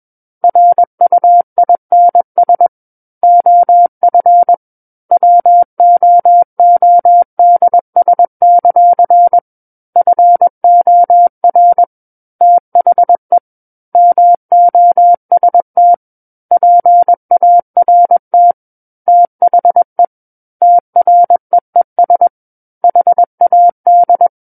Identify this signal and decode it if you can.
Morse code, heard as a keyed tone, about 21 wpm, 700 Hz. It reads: RUINS OF WOODS; FOR THE MOST PART THE TREES HAD
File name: Book One / 12-Chapter 12 - 21 WPM